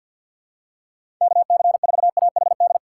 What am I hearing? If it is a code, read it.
Morse code, heard as a keyed tone, about 51 wpm, 700 Hz. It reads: KC4AHD
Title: USA Call Signs - 6 Characters - 3x Character Spacing - 50wpm